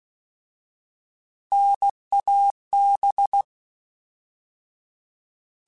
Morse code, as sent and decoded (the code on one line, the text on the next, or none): -. .- -...
NAB